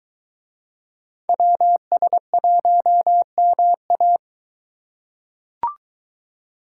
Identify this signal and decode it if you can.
Morse code, heard as a keyed tone, about 22 wpm, 700 Hz. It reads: WS1MA